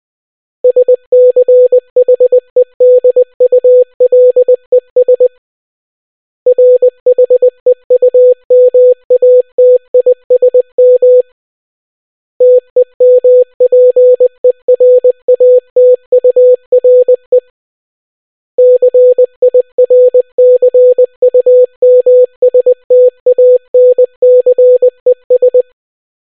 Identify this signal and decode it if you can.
Morse code, heard as a keyed tone, about 20 wpm, 500 Hz. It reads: SCHEDULES RHEUMATISM TEMPERATURE CIRCUMSTANCES